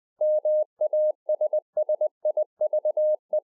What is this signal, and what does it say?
Morse code, heard as a keyed tone, about 20 wpm, 610 Hz. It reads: MASSIVE